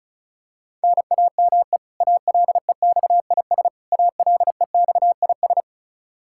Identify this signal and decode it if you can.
Morse code, heard as a keyed tone, about 36 wpm, 700 Hz. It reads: NAME ALEXIS ALEXIS
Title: Practice QSO - Exchange 1 - Beginning - 35wpm